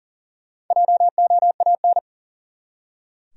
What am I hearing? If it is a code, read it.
Morse code, heard as a keyed tone, about 40 wpm, 700 Hz. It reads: JOAN
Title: USA Common Names - Max 5 Letters - 40wpm